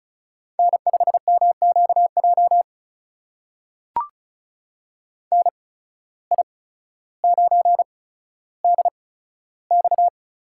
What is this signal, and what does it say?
Morse code, heard as a keyed tone, about 34 wpm, spaced as for 15 wpm, 700 Hz. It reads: N5MQJ NI9DX
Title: USA Call Signs - 5 Characters - 8x Character Spacing - 35wpm